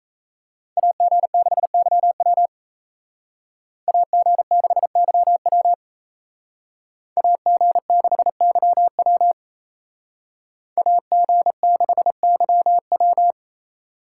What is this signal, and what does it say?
Morse code, heard as a keyed tone, about 36 wpm, 700 Hz. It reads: AG6YW AG6YW AG6YW AG6YW